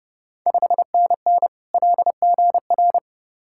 Morse code, heard as a keyed tone, about 30 wpm, 700 Hz. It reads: 5NN LGR
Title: Ontario QSO Party Exchange 30wpm